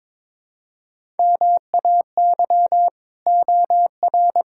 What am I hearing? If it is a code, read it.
Morse code, heard as a keyed tone, about 22 wpm, 700 Hz. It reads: MAY OR
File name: practice / 22wpm / Build A Phrase - 5-Word Phrases - 22wpm